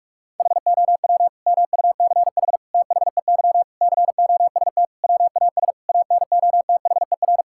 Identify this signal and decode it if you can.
Morse code, heard as a keyed tone, about 41 wpm, 700 Hz. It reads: HOW MUCH THEY COST WAS ANOTHER
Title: Sentences from Top 1000 Words - Review Part 5 - 45wpm